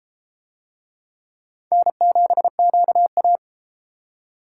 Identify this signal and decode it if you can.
Morse code, heard as a keyed tone, about 31 wpm, 700 Hz. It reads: N7QA